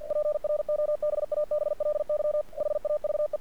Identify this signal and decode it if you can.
Morse code, heard as a keyed tone, about 48 wpm, 600 Hz. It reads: PROBABLY HAVE
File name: archived / CRSnet